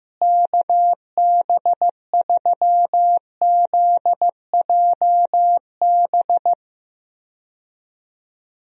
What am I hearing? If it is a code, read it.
Morse code, heard as a keyed tone, about 15 wpm, 700 Hz. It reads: KB3ZJB
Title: USA Call Signs - 6 Characters - Repeated 3x - 15wpm